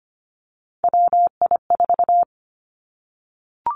Morse code, heard as a keyed tone, about 25 wpm, 700 Hz. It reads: WI4
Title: Three Challenging Letter-Number Combinations 25wpm